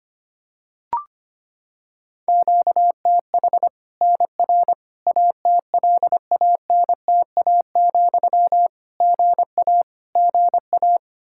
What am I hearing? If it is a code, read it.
Morse code, heard as a keyed tone, about 25 wpm, 700 Hz. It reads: QTH NR ATLANTA, GA GA
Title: K1USN SST QSO Practice 25wpm